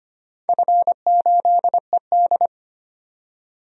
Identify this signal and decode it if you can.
Morse code, heard as a keyed tone, about 25 wpm, 700 Hz. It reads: F8ED
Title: DXCC-100 Call Signs - 4 Characters - 25wpm